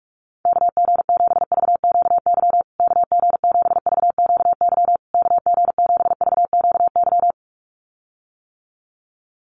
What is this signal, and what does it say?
Morse code, heard as a keyed tone, about 44 wpm, 700 Hz. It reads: KG7VQY KG7VQY KG7VQY
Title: USA Call Signs Repeated 3x 45wpm